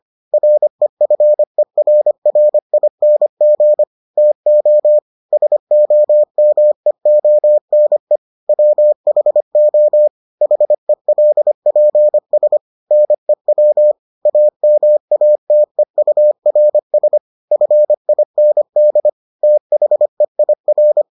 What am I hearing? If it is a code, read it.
Morse code, heard as a keyed tone, about 25 wpm, 600 Hz. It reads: REFERRING TO SOMEONE WHO HELPS NEW AMATEURS FIND THEIR